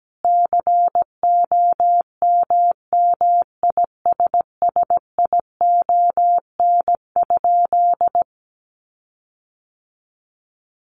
Morse code, heard as a keyed tone, about 17 wpm, 700 Hz. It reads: COMMISSION?
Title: Sentences from Top 1400 Words 17wpm